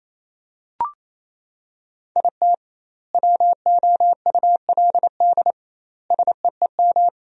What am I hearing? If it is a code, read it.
Morse code, heard as a keyed tone, about 28 wpm, 700 Hz. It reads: IT WOULD SEEM
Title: 2x Word-Spacing - Sentences from Top 300 Words - 28wpm